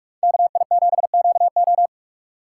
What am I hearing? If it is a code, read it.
Morse code, heard as a keyed tone, about 44 wpm, 700 Hz. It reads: KI7QO